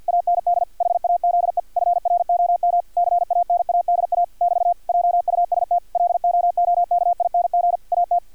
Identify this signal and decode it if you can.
Morse code, heard as a keyed tone, about 50 wpm, 700 Hz, with radio noise. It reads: AND HAZE FROM CANADA <BT> JUST LOOKING AT
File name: archived / CRSnet